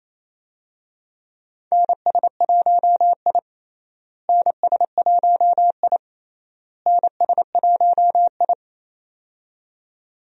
Morse code, heard as a keyed tone, about 27 wpm, 700 Hz. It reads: NS1I NS1I NS1I
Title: USA Call Signs - 4 Characters - Repeated 3x - 28wpm